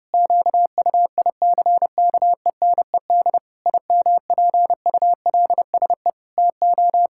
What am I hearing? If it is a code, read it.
Morse code, heard as a keyed tone, about 30 wpm, 700 Hz. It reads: QUICKENED IMPULSE TO
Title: Pangram Challenge v2 - 30wpm